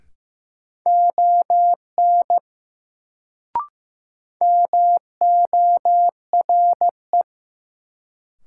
Morse code, heard as a keyed tone, about 15 wpm, 700 Hz. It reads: ON MORE
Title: Sentences from Top 100 Words - New Word Drill - 15wpm